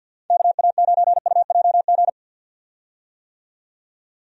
Morse code, heard as a keyed tone, about 51 wpm, 700 Hz. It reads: KA9UJG
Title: USA Call Signs - 6 Characters - 50wpm